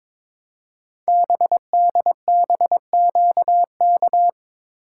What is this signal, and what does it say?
Morse code, heard as a keyed tone, about 23 wpm, 700 Hz. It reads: BDBQK
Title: Code Groups - 5 Letters - 22wpm